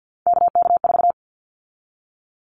Morse code, heard as a keyed tone, about 50 wpm, 700 Hz. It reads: KK4
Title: Callsign Prefixes for Top 25 DXCC - 50wpm